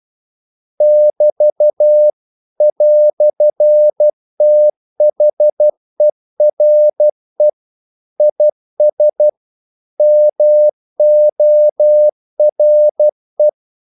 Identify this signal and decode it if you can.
Morse code, heard as a keyed tone, about 12 wpm, 600 Hz. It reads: <BT> "THERE IS MORE
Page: English quotations in CW (telegraphy - morse)